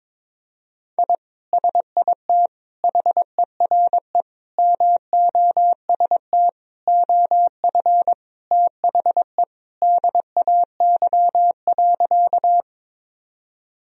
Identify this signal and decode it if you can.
Morse code, heard as a keyed tone, about 22 wpm, 700 Hz. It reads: I SIT HERE MOST OF THE DAY.
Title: Sentences from Top 1000 Words - Review Part 1 - 22wpm